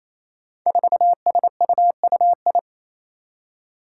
Morse code, heard as a keyed tone, about 28 wpm, 700 Hz. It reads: 4SUUI
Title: Five Challenging Letter-Number Combinations 28wpm